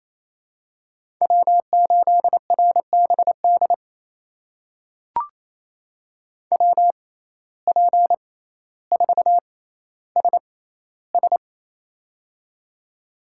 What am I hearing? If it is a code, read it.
Morse code, heard as a keyed tone, about 28 wpm, spaced as for 14 wpm, 700 Hz. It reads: W8RBD WP4SS
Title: USA Call Signs - 5 Characters - 6x Character Spacing - 28wpm